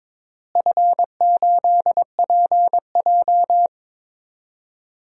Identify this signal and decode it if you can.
Morse code, heard as a keyed tone, about 21 wpm, 700 Hz. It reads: F8PJ
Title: DXCC-100 Call Signs - 4 Characters - Repeated 2x - 22wpm